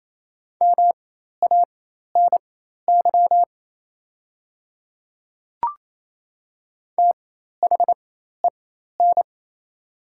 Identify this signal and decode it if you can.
Morse code, heard as a keyed tone, about 27 wpm, spaced as for 13 wpm, 700 Hz. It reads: MANY THEN
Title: 4x Character Spacing - Top 300 Words - 28wpm